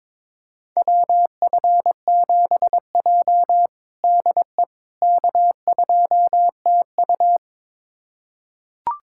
Morse code, heard as a keyed tone, about 21 wpm, 700 Hz. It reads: WF7J DE K2TU
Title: USA Callsign DE Callsign - 4 Characters - 22wpm